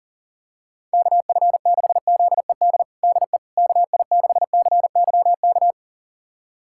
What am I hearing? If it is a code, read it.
Morse code, heard as a keyed tone, about 40 wpm, 700 Hz. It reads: KF6ZED DE KI6CYK
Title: USA Callsign DE Callsign - 6 Characters - 40wpm